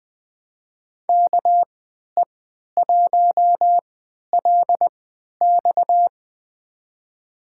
Morse code, heard as a keyed tone, about 20 wpm, spaced as for 11 wpm, 700 Hz. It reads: KE1LX
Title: USA Call Signs - 3x Character Spacing - 20wpm